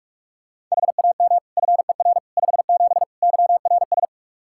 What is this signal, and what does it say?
Morse code, heard as a keyed tone, about 42 wpm, 700 Hz. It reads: HAM FER 57 YRS
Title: K1USN SST QSO Practice 45wpm